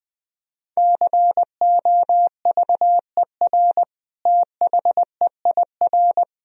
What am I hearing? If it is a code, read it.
Morse code, heard as a keyed tone, about 20 wpm, 700 Hz. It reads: COVER THEIR
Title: Sentences from Top 1000 Words - Review Part 1 - 20wpm